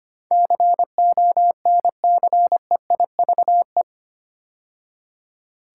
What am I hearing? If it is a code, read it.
Morse code, heard as a keyed tone, about 25 wpm, 700 Hz. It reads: CONCEIVE
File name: 8 Letter Words - First 3 Letters Given - 25wpm